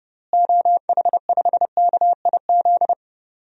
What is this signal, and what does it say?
Morse code, heard as a keyed tone, about 30 wpm, 700 Hz. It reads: OH5KIZ
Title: DXCC-100 Call Signs - 6 Characters - Repeated 3x - 30wpm